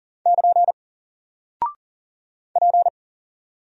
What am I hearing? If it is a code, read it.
Morse code, heard as a keyed tone, about 41 wpm, 700 Hz. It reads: <KN> P
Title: Single Character (Letter:Number:Punctuation:Prosign) 40wpm